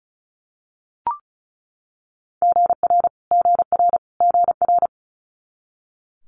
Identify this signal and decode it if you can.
Morse code, heard as a keyed tone, about 36 wpm, 700 Hz. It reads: GR GR GR
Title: Multi-letter Phonemes Repeated 3x 35wpm